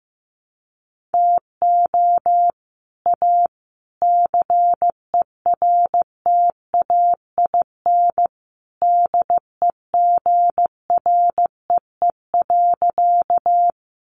Morse code, heard as a keyed tone, about 15 wpm, 700 Hz. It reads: TO A CERTAIN DEGREE.
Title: Sentences from Top 1500 Words 15wpm